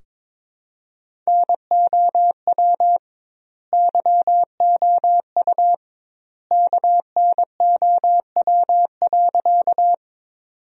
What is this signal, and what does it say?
Morse code, heard as a keyed tone, about 22 wpm, 700 Hz. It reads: NOW YOU KNOW.
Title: 2x Word-Spacing - Sentences from Top 100 Words - 22wpm